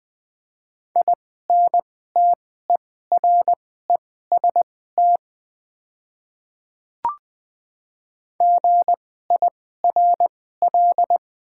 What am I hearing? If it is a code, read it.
Morse code, heard as a keyed tone, about 20 wpm, spaced as for 14 wpm, 700 Hz. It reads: INTEREST GIRL
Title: 2x Character Spacing - Top 500 Words - 20wpm